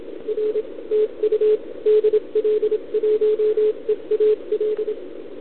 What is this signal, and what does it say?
Morse code, heard as a keyed tone, about 27 wpm, 440 Hz, with radio noise. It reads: R TU DL1EAL